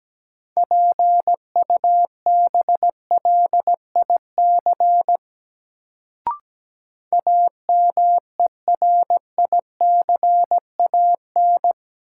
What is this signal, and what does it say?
Morse code, heard as a keyed tone, about 17 wpm, 700 Hz. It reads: PUBLIC AMERICAN
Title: Sets of 2 Words - New Word Drill - 17wpm